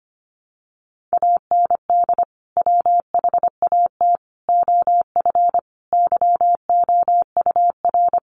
Morse code, heard as a keyed tone, about 25 wpm, 700 Hz. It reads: AND WHAT OF YOUR
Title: Sentences from Top 900 Words 25wpm